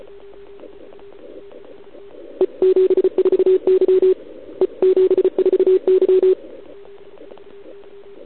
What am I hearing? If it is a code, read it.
Morse code, heard as a keyed tone, about 34 wpm, 370 Hz. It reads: E74Y E74Y
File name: ARCHIV / OKDX2019 / OKOM40 KX3